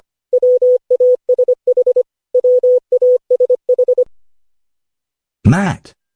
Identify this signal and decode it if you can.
Morse code, heard as a keyed tone, about 25 wpm, 500 Hz, with speech. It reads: WASH WASH
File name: cw / Session5 cw training